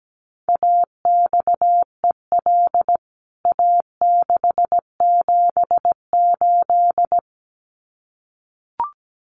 Morse code, heard as a keyed tone, about 17 wpm, 700 Hz. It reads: AXEL A678
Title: CWT Exchange 17wpm